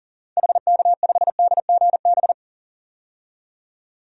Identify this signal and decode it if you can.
Morse code, heard as a keyed tone, about 40 wpm, 700 Hz. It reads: HK5DGB
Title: DXCC-100 Call Signs - 6 Characters - 40wpm